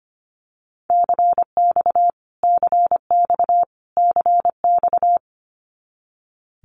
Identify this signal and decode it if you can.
Morse code, heard as a keyed tone, about 25 wpm, 700 Hz. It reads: CX CX CX
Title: Abbreviations Repeated 3x - 25wpm